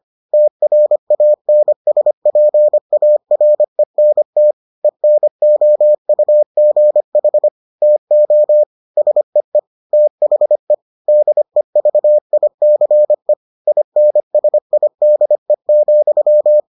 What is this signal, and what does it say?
Morse code, heard as a keyed tone, about 25 wpm, 600 Hz. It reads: TRANSPARENT ENOUGH TO SEE THE DEVICE INSIDE,